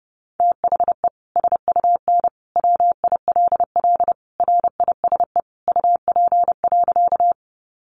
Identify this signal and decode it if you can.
Morse code, heard as a keyed tone, about 30 wpm, 700 Hz. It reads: THE SUN WILL RISE UP.
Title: Sentences from Top 900 Words 30wpm